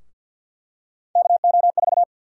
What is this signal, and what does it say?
Morse code, heard as a keyed tone, about 52 wpm, 700 Hz. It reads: KO4
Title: USA Call Sign Prefixes - 50wpm